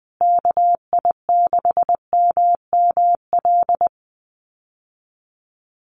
Morse code, heard as a keyed tone, about 20 wpm, 700 Hz. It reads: KI6MML
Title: USA Call Signs Repeated 3x 20wpm